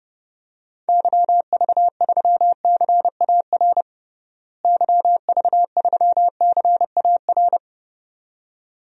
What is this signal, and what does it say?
Morse code, heard as a keyed tone, about 32 wpm, 700 Hz. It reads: YV3CAR YV3CAR
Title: DXCC-100 Call Signs - 6 Characters - Repeated 2x - 30wpm